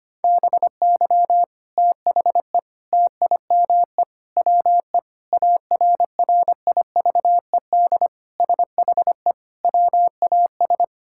Morse code, heard as a keyed tone, about 25 wpm, 700 Hz. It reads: BY THE TIME WE ARRIVED SHE WAS
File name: practice / 25wpm / Sentences with 3-Word Phrases - 25wpm